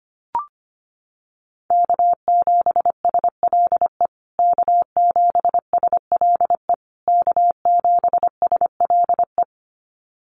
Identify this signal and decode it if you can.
Morse code, heard as a keyed tone, about 25 wpm, 700 Hz. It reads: K7SLE K7SLE K7SLE
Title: USA Call Signs Repeated 3x 25wpm